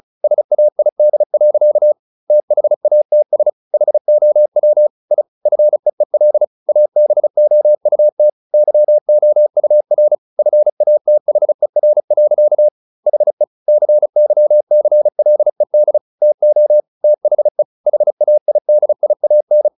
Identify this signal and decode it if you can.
Morse code, heard as a keyed tone, about 35 wpm, 600 Hz. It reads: SAID. THATS HOW I FEEL ABOUT YOUR FATHER. HE CYCLED TO THE HAIDIAN